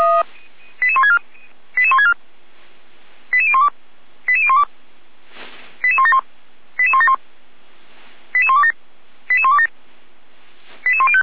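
Five-tone selective call (ZVEI), heard as quick runs of five tones, tones of about 70 ms. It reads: 88164 88164 88012 88012 88171 88171 88127 88127 88105
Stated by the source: Five-tone ZVEI selective call 88156, received by SDR.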